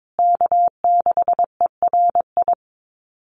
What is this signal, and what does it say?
Morse code, heard as a keyed tone, about 22 wpm, 700 Hz. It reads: K6ERI
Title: USA Call Signs Repeated 3x 22wpm